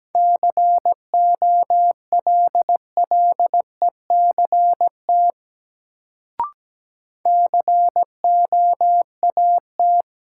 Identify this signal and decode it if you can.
Morse code, heard as a keyed tone, about 17 wpm, 700 Hz. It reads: COLLECT COAT
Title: Sentences from Top 1600 Words - New Word Drill Alt - 17wpm